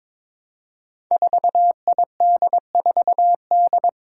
Morse code, heard as a keyed tone, about 22 wpm, 700 Hz. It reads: 4ID4D
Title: Five Challenging Letter-Number Combinations 22wpm